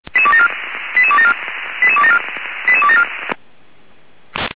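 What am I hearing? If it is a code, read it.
Five-tone selective call (ZVEI), heard as quick runs of five tones, tones of about 70 ms. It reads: 88274 88274 88274
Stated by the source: ZVEI five-tone call received with an SDR.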